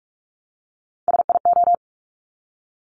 Morse code, heard as a keyed tone, about 46 wpm, 690 Hz. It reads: SIO
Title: Top Three Letter Combinations in Words 45wpm